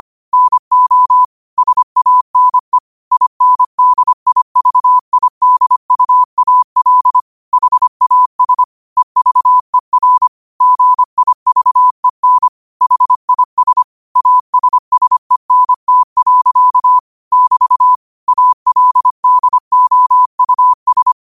Morse code, heard as a keyed tone, about 25 wpm, 1000 Hz. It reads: NO SANE INDIVIDUAL HAS EVER GIVEN HIS ASSENT. <BT> ALDOUS